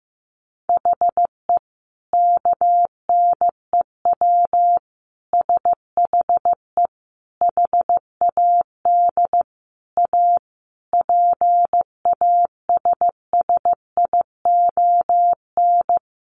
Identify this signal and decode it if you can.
Morse code, heard as a keyed tone, about 15 wpm, 700 Hz. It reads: HE KNEW SHE HAD A PASSION
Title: Sentences from Top 1300 Words 15wpm